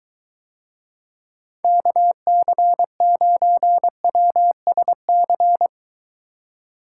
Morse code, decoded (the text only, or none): KC9WSC